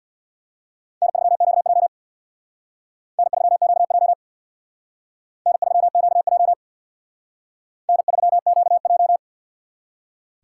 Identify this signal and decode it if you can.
Morse code, heard as a keyed tone, about 64 wpm, 700 Hz. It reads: N3QJ N3QJ N3QJ N3QJ